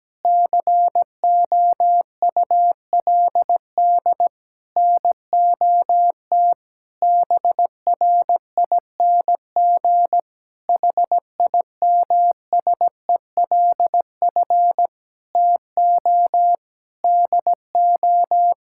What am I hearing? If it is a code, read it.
Morse code, heard as a keyed tone, about 17 wpm, 700 Hz. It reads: COULD NOT BRING HIMSELF TO DO
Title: Sentences from Top 1600 Words 17wpm